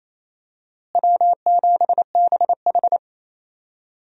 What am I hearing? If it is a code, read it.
Morse code, heard as a keyed tone, about 28 wpm, 700 Hz. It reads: W7BH
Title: USA Call Signs - 4 Characters - Repeated 2x - 28wpm